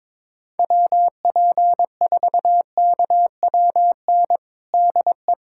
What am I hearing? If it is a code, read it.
Morse code, heard as a keyed tone, about 22 wpm, 700 Hz. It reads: WP4KWN DE
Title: USA Callsign DE Callsign - 6 Characters - 22wpm